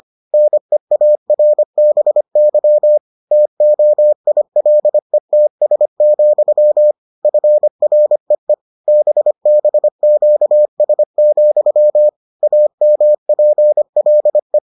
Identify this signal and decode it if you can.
Morse code, heard as a keyed tone, about 25 wpm, 600 Hz. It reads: NEARBY TOILETS, FREE BBQS, AMPLE